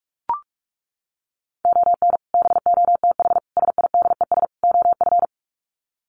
Morse code, heard as a keyed tone, about 44 wpm, 700 Hz. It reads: ON BOTH SIDES OF
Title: Sets of 4 Words 45wpm